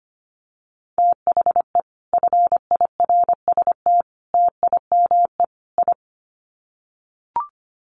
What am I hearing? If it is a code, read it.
Morse code, heard as a keyed tone, about 25 wpm, 700 Hz. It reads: THE FIRST TIME I
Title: Sets of 4 Words 25wpm